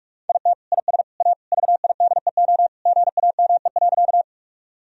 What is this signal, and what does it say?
Morse code, heard as a keyed tone, about 46 wpm, 700 Hz. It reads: IT IS A VIDEO GAME.
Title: Sentences from Top 1000 Words - Review Part 4 - 45wpm